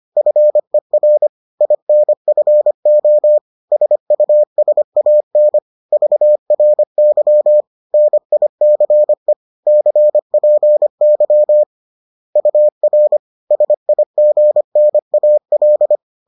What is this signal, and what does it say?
Morse code, heard as a keyed tone, about 25 wpm, 600 Hz. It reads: FER INFO SUSAN VRY NICE CPY UR SIGNAL